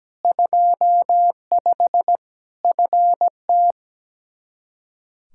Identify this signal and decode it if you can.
Morse code, heard as a keyed tone, about 17 wpm, 700 Hz. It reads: 25 FT
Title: K1USN SST QSO Practice 17wpm